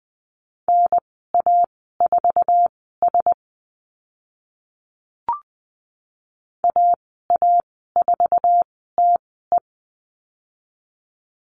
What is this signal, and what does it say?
Morse code, heard as a keyed tone, about 20 wpm, spaced as for 14 wpm, 700 Hz. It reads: NA4S AA4TE
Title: 055 Lesson 10g - Letter r - Call Signs 20wpm